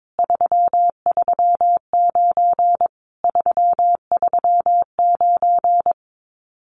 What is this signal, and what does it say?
Morse code, heard as a keyed tone, about 22 wpm, 700 Hz. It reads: 339 339
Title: Common QSO Phrases Encore 22wpm